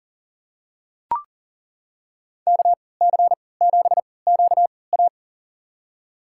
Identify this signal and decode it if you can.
Morse code, heard as a keyed tone, about 40 wpm, spaced as for 23 wpm, 700 Hz. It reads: KC7QA
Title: USA Call Signs - 5 Characters - 3x Character Spacing - 40wpm